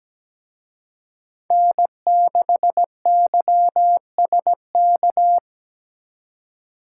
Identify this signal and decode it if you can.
Morse code, heard as a keyed tone, about 20 wpm, 700 Hz. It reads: N6YSK
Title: Speed Racing - USA 1x3 Call Signs 17wpm